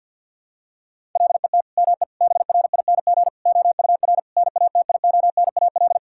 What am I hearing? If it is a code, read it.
Morse code, heard as a keyed tone, about 51 wpm, 700 Hz. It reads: LET ME BRING OUR NATIONAL
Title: Sentences from Top 1000 Words - Review Part 6 - 50wpm